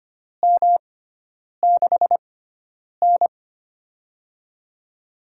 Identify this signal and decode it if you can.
Morse code, heard as a keyed tone, about 25 wpm, spaced as for 9 wpm, 700 Hz. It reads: M6N
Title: USA Call Signs - 6x Character Spacing - 25wpm